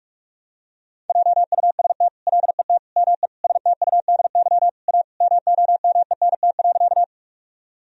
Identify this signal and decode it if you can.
Morse code, heard as a keyed tone, about 46 wpm, 700 Hz. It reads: JUST LET ME STUDY A MOMENT.
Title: Sentences from Top 1000 Words - Review Part 1 - 45wpm